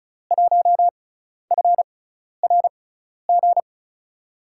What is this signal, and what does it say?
Morse code, heard as a keyed tone, about 34 wpm, spaced as for 12 wpm, 700 Hz. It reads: 1FRG
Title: USA Call Signs - 6 Characters - 6x Character Spacing - 35wpm